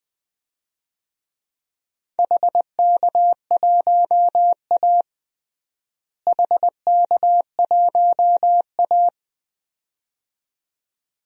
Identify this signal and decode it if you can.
Morse code, heard as a keyed tone, about 20 wpm, 700 Hz. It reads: HK1A HK1A
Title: DXCC-100 Call Signs - 4 Characters - Repeated 2x - 20wpm